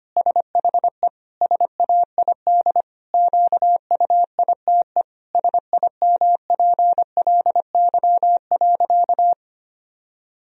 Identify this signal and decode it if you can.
Morse code, heard as a keyed tone, about 25 wpm, 700 Hz. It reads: SHE SAID QUITE SIMPLY.
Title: Sentences from Top 1000 Words - Review Part 4 - 25wpm